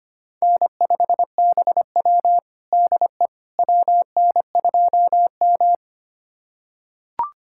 Morse code, heard as a keyed tone, about 25 wpm, 700 Hz. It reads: N5BW DE WN2M
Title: USA Callsign DE Callsign - 4 Characters - 25wpm